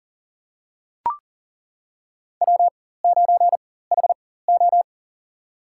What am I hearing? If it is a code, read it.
Morse code, heard as a keyed tone, about 39 wpm, spaced as for 19 wpm, 690 Hz. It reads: W9HO
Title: USA Call Signs - 4x Character Spacing - 40wpm